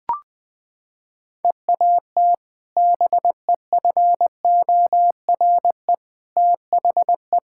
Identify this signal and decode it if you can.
Morse code, heard as a keyed tone, about 20 wpm, 700 Hz. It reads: EAT BEFORE THE
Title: Sentences with 2-Word Phrases - 20wpm